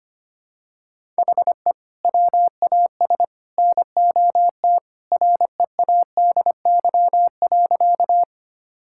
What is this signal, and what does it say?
Morse code, heard as a keyed tone, about 25 wpm, 700 Hz. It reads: HE WAS NOT READY.
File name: practice / 25wpm / Sentences with 3-Word Phrases - 25wpm